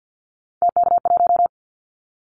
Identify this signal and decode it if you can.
Morse code, heard as a keyed tone, about 50 wpm, 700 Hz. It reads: TK1
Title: Callsign Prefixes for Top 100 DXCC - 50wpm